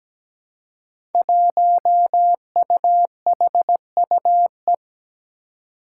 Morse code, heard as a keyed tone, about 17 wpm, 700 Hz. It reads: 1UHUE
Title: Five Challenging Letter-Number Combinations 17wpm